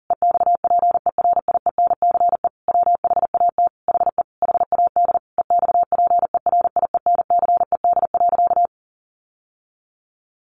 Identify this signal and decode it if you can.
Morse code, heard as a keyed tone, about 40 wpm, 700 Hz. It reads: EXPERIENCE WHAT HE HAD EXPERIENCED.
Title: Sentences from Top 1100 Words 40wpm